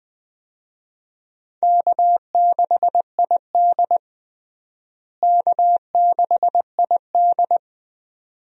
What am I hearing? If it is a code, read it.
Morse code, heard as a keyed tone, about 21 wpm, 700 Hz. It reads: K6ID K6ID
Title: USA Call Signs - 4 Characters - Repeated 2x - 20wpm